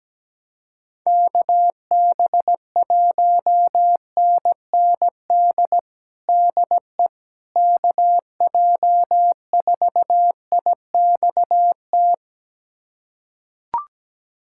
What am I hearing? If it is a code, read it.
Morse code, heard as a keyed tone, about 17 wpm, 700 Hz. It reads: KB1NND DE KJ4IXT
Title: USA Callsign DE Callsign - 6 Characters - 17wpm